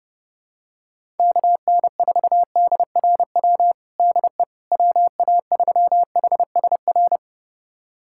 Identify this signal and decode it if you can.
Morse code, heard as a keyed tone, about 30 wpm, 700 Hz. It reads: KN4DRW DE WA3HSR
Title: USA Callsign DE Callsign 30wpm